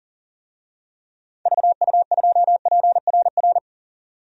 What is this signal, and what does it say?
Morse code, heard as a keyed tone, about 40 wpm, 700 Hz. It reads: VU2PRR
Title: DXCC-100 Call Signs - 6 Characters - 40wpm